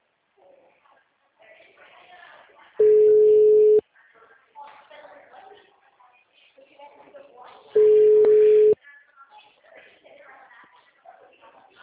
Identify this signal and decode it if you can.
Ringback tone (Europe, 425 Hz), with speech.
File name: out- -2022- - - .34399